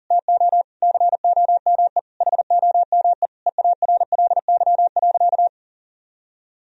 Morse code, heard as a keyed tone, about 40 wpm, 700 Hz. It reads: TO COME HOME EARLY.
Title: Sentences from Top 1000 Words - Review Part 4 - 40wpm